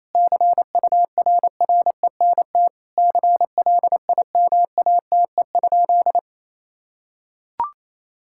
Morse code, heard as a keyed tone, about 28 wpm, 700 Hz. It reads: CURRENT CLIMATE?
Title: Sentences from Top 1600 Words 28wpm